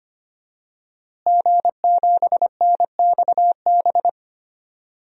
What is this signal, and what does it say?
Morse code, heard as a keyed tone, about 25 wpm, 700 Hz. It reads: G7NXB
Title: DXCC-100 Call Signs - 5 Characters - Repeated 2x - 25wpm